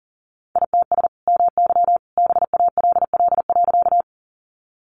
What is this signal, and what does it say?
Morse code, heard as a keyed tone, about 40 wpm, 700 Hz. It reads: ITS MY BALL.
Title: Sentences from Top 900 Words 40wpm